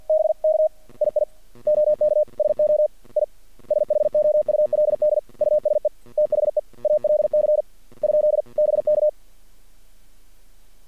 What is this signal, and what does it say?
Morse code, heard as a keyed tone, about 50 wpm, 610 Hz. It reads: BK II OK SO I SHOULD USE THE ALC ? BK
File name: archived / CRSnet